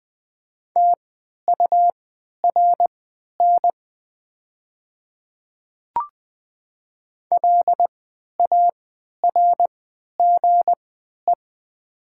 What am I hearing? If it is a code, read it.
Morse code, heard as a keyed tone, about 20 wpm, spaced as for 11 wpm, 700 Hz. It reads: TURN LARGE
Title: 3x Character Spacing - Top 200 Words - 20wpm